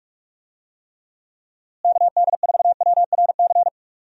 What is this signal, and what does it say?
Morse code, heard as a keyed tone, about 44 wpm, 700 Hz. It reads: KD4WRC
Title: USA Call Signs - 6 Characters - Repeated 3x - 45wpm